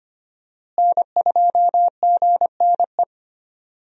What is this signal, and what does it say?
Morse code, heard as a keyed tone, about 25 wpm, 700 Hz. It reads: N2GNE
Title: USA Call Signs - 5 Characters - Repeated 3x - 25wpm